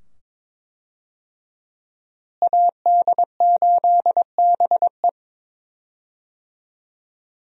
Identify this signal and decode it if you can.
Morse code, heard as a keyed tone, about 22 wpm, 700 Hz. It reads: AD8BE
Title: USA Call Signs - 5 Characters - Repeated 2x - 22wpm